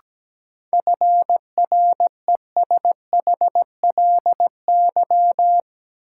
Morse code, heard as a keyed tone, about 17 wpm, 700 Hz. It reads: FRESHLY
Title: Sets of 5 Words Encore - New Word Drill - 17wpm